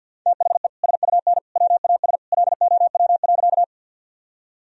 Morse code, heard as a keyed tone, about 51 wpm, 700 Hz. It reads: THE SUN WAS LOW.
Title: Sentences from Top 1000 Words - Review Part 1 - 50wpm